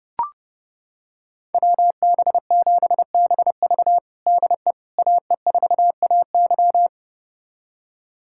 Morse code, heard as a keyed tone, about 30 wpm, 700 Hz. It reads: WB7BV DE AE4AY
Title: USA Callsign DE Callsign - 5 Characters - 30wpm